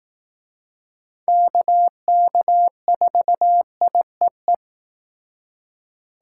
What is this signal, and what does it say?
Morse code, heard as a keyed tone, about 18 wpm, 700 Hz. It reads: KK4IEE
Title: Speed Racing - USA 2x3 Call Signs 15wpm